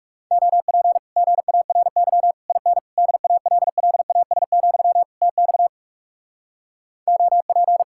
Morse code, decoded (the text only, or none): OP GARY IN DALLAS, TX OP